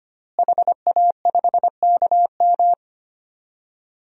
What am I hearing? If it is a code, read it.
Morse code, heard as a keyed tone, about 25 wpm, 700 Hz. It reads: HA5KM
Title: DXCC-100 Call Signs - 5 Characters - Repeated 2x - 25wpm